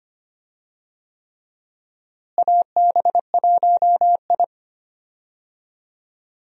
Morse code, heard as a keyed tone, about 25 wpm, 700 Hz. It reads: AB1I
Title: USA Call Signs - 4 Characters - Repeated 3x - 25wpm